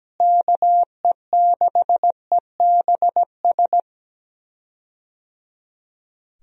Morse code, heard as a keyed tone, about 17 wpm, 700 Hz. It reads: KE6EBS
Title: USA Call Signs - 6 Characters - Repeated 3x - 17wpm